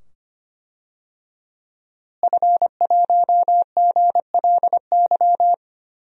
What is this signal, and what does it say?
Morse code, heard as a keyed tone, about 25 wpm, 700 Hz. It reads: F1GLY
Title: DXCC-100 Call Signs - 5 Characters - Repeated 2x - 25wpm